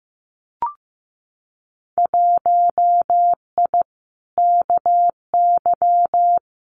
Morse code, heard as a keyed tone, about 15 wpm, 700 Hz. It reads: 1I KY
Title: Winter Field Day Exchanges 15wpm